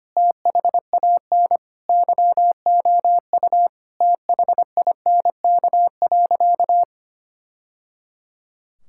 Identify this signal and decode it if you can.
Morse code, heard as a keyed tone, about 25 wpm, 700 Hz. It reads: THAN YOU THINK.
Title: Sentences from Top 1000 Words - Review Part 6 - 25wpm